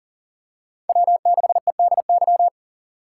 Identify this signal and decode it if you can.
Morse code, heard as a keyed tone, about 40 wpm, 700 Hz. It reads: W6EDY
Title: K1USN SST QSO Practice - 40wpm